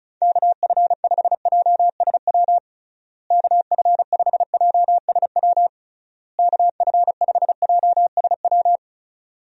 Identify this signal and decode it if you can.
Morse code, heard as a keyed tone, about 34 wpm, 700 Hz. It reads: KF5JSW KF5JSW KF5JSW
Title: USA Call Signs - 6 Characters - Repeated 3x - 35wpm